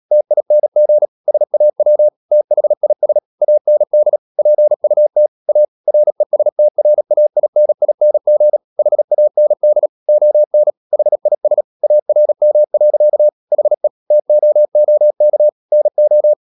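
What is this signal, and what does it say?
Morse code, heard as a keyed tone, about 37 wpm, 600 Hz. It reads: TING SAW THIS AND PUT A RESTRAINING HAND ON HIS ARM. HE TOOK NO
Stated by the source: Morse code practice at 35–39 wpm sending English text of Ken Follett's novel Never, from CW med Gnister.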